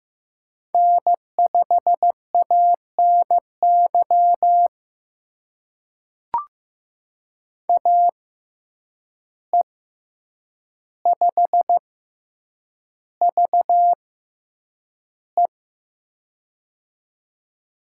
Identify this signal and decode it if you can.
Morse code, heard as a keyed tone, about 15 wpm, spaced as for 8 wpm, 700 Hz. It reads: N5ANY AE5VE
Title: USA Call Signs - 5 Characters - 6x Character Spacing - 15wpm